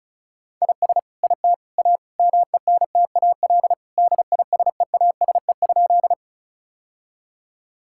Morse code, heard as a keyed tone, about 36 wpm, 700 Hz. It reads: IS IT A MENTAL DISEASE?
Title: Sentences from Top 1000 Words - Review Part 6 - 35wpm